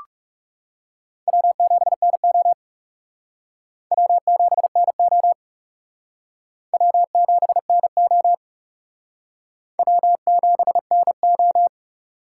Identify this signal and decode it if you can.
Morse code, heard as a keyed tone, about 37 wpm, 700 Hz. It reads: W7NO W7NO W7NO W7NO
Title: Speed Racing - USA 1x2 Call Signs 30wpm